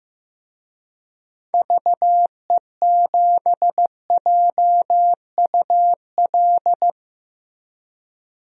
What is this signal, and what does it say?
Morse code, heard as a keyed tone, about 15 wpm, 700 Hz. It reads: VE7JUL
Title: DXCC-100 Call Signs - 6 Characters - Repeated 3x - 15wpm